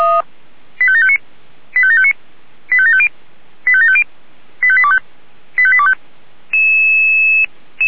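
Five-tone selective call (ZVEI), heard as quick runs of five tones, tones of about 70 ms. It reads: 85759 85759 85750 85750 85825 85825
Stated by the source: ZVEI five-tone selective call received by SDR.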